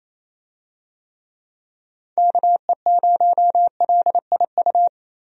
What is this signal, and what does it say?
Morse code, heard as a keyed tone, about 28 wpm, 700 Hz. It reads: KE0LIU
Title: USA Call Signs - 6 Characters - 28wpm